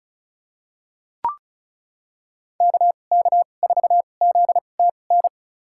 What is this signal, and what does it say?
Morse code, heard as a keyed tone, about 34 wpm, spaced as for 25 wpm, 700 Hz. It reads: KK4ZTN
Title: USA Call Signs - 6 Characters - 2x Character Spacing - 35wpm